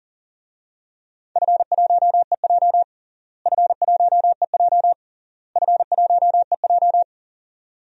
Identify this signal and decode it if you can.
Morse code, heard as a keyed tone, about 40 wpm, 700 Hz. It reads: F1EJ F1EJ F1EJ
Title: DXCC-100 Call Signs - 4 Characters - Repeated 3x - 40wpm